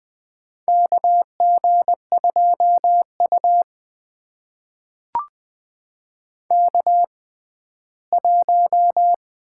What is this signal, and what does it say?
Morse code, heard as a keyed tone, about 20 wpm, spaced as for 14 wpm, 700 Hz. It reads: KG2U K1
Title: USA Call Signs - 4 Characters - 6x Character Spacing - 20wpm